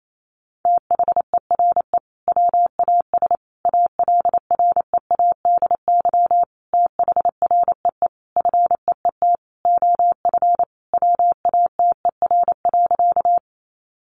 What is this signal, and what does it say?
Morse code, heard as a keyed tone, about 28 wpm, 700 Hz. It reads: THERE WAS ALREADY THREE FEET OF WATER.
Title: Sentences from Top 700 Words 28wpm